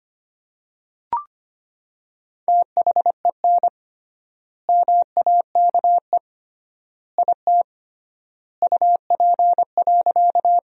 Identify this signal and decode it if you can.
Morse code, heard as a keyed tone, about 25 wpm, 700 Hz. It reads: THEN MAKE IT UP.
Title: 3x Word-Spacing - Sentences from Top 100 Words - 25wpm